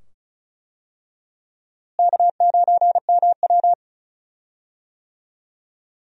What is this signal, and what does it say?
Morse code, heard as a keyed tone, about 36 wpm, 700 Hz. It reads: K9MW
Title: USA Call Signs - 4 Characters - Repeated 2x - 35wpm